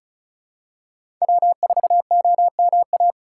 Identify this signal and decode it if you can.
Morse code, heard as a keyed tone, about 36 wpm, 700 Hz. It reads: W4OMA
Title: USA Call Signs - 2x Character Spacing - 35wpm